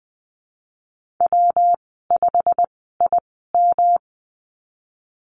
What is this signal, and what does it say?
Morse code, heard as a keyed tone, about 20 wpm, spaced as for 15 wpm, 700 Hz. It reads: W5IM
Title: 122 Lesson 20g - Number 3 - Call Signs 20wpm